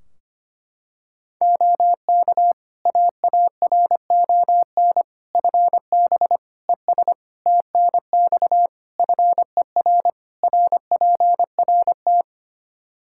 Morse code, heard as a keyed tone, about 25 wpm, 700 Hz. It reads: OK AARON FB ES TNX FER RPRT
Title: Practice QSO - Exchange 2 - Info - 25wpm